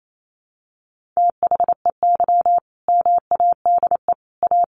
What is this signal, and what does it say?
Morse code, heard as a keyed tone, about 28 wpm, 700 Hz. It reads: THEY MADE A
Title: Sentences from Top 1300 Words 28wpm